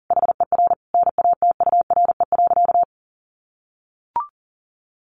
Morse code, heard as a keyed tone, about 40 wpm, 700 Hz. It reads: HER NATURE.
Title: Sentences from Top 1300 Words 40wpm